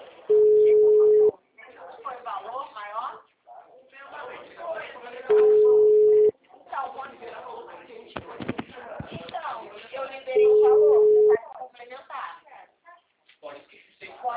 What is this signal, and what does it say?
Ringback tone (Europe, 425 Hz), with speech.